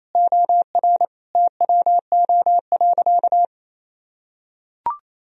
Morse code, heard as a keyed tone, about 28 wpm, 700 Hz. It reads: OR TWO.
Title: Sentences from Top 1000 Words - Review Part 1 - 28wpm